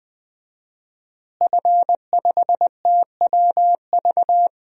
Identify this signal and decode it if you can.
Morse code, heard as a keyed tone, about 21 wpm, 700 Hz. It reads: F5TWV
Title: DXCC-100 Call Signs - 5 Characters - Repeated 3x - 20wpm